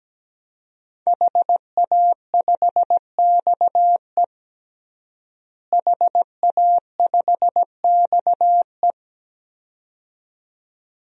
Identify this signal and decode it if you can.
Morse code, heard as a keyed tone, about 17 wpm, 700 Hz. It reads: HA5XE HA5XE
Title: DXCC-100 Call Signs - 5 Characters - Repeated 2x - 17wpm